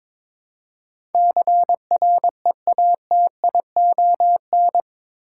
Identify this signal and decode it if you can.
Morse code, heard as a keyed tone, about 22 wpm, 700 Hz. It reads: CREATION
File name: practice / 22wpm / Eight Letter Words - 22wpm